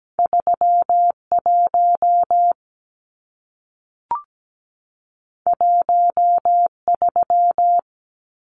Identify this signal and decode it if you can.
Morse code, heard as a keyed tone, about 17 wpm, 700 Hz. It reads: 31 13
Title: Numbers - 2-Digit 17wpm